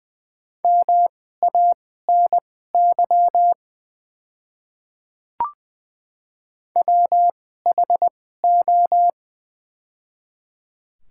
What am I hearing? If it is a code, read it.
Morse code, heard as a keyed tone, about 20 wpm, spaced as for 15 wpm, 690 Hz. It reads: MANY WHO
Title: 2x Character Spacing - Top 300 Words - 20wpm